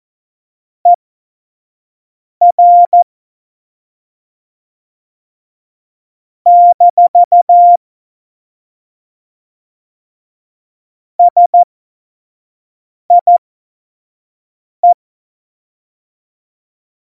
Morse code, heard as a keyed tone, about 14 wpm, spaced as for 5 wpm, 700 Hz. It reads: ER - SIE
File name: Morsekurs / Witze / Witz 23. Januar 2018, 25 F70